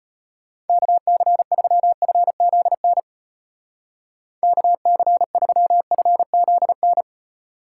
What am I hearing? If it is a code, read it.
Morse code, heard as a keyed tone, about 35 wpm, 700 Hz. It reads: KC3FZN KC3FZN